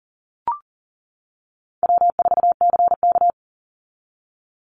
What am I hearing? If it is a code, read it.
Morse code, heard as a keyed tone, about 40 wpm, 700 Hz. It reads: W4CK
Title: Call Signs - 40wpm